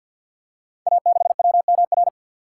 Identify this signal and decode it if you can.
Morse code, heard as a keyed tone, about 48 wpm, 700 Hz. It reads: A6WMR
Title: Code Groups - 5 Letter:Numbers - 50wpm